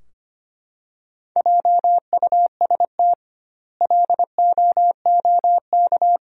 Morse code, heard as a keyed tone, about 25 wpm, 700 Hz. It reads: JUST LOOK
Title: 2x Word-Spacing - Sentences from Top 200 Words - 25wpm